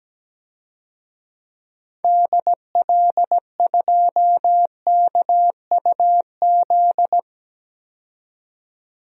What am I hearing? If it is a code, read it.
Morse code, heard as a keyed tone, about 17 wpm, 700 Hz. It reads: DL2KUZ
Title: DXCC-100 Call Signs - 6 Characters - 17wpm